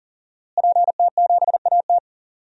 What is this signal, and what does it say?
Morse code, heard as a keyed tone, about 40 wpm, 700 Hz. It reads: PT7AT
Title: DXCC-100 Call Signs - 5 Characters - Repeated 2x - 40wpm